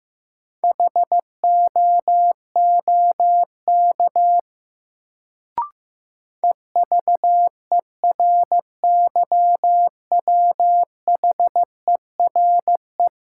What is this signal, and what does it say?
Morse code, heard as a keyed tone, about 15 wpm, 700 Hz. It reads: HOOK EVERYWHERE
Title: Sentences from Top 1600 Words - New Word Drill - 15wpm